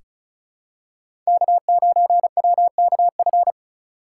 Morse code, heard as a keyed tone, about 34 wpm, 690 Hz. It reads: K9WKF
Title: USA Call Signs - 5 Characters - 3x Character Spacing - 35wpm